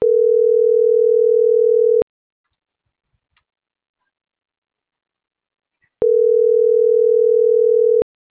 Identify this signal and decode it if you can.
Ringback tone (North America), 440 + 480 Hz.